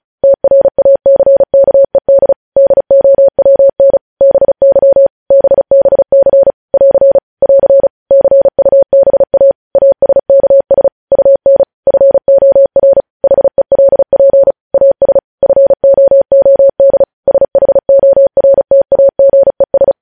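Morse code, heard as a keyed tone, about 36 wpm, 550 Hz. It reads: TRACKED DOWN BY BBC <AR> <AR> CUBA ASKS UN FOR HELP AS FOOD SHORTAGES
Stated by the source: Morse code practice sending news headlines at 35 wpm.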